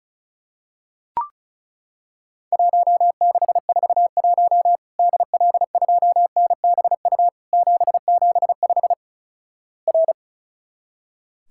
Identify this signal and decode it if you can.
Morse code, heard as a keyed tone, about 36 wpm, 700 Hz. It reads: 1641 DL2NBU 775 R
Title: Worked All Europe DX Contest - QTC Exchange - 35wpm